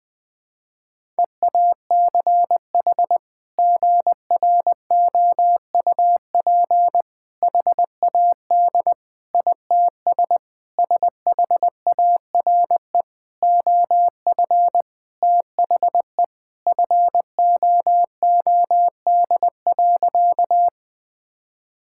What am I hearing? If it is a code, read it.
Morse code, heard as a keyed tone, about 20 wpm, 700 Hz. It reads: EACH GROUP HAD ITS SHARE OF THE FOOD.
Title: Sentences from Top 1000 Words - Review Part 6 - 20wpm